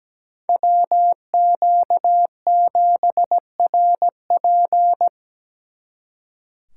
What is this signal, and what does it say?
Morse code, heard as a keyed tone, about 20 wpm, 700 Hz. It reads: WQ7RP
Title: Speed Racing - USA 2x2 Call Signs 17wpm